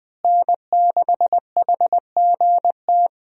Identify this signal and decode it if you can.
Morse code, heard as a keyed tone, about 20 wpm, 700 Hz. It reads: N6HGT
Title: USA Call Signs - 5 Characters - Repeated 3x - 20wpm